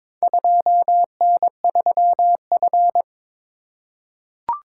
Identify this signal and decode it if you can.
Morse code, heard as a keyed tone, about 22 wpm, 700 Hz. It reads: 2N3F
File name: practice / 22wpm / Four Letter-Number Combinations - 22wpm